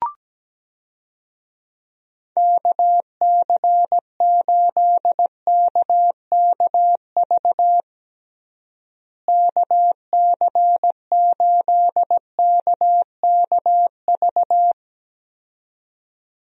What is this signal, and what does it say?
Morse code, heard as a keyed tone, about 17 wpm, 700 Hz. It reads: KC8KKV KC8KKV
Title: USA Call Signs - 6 Characters - Repeated 2x - 17wpm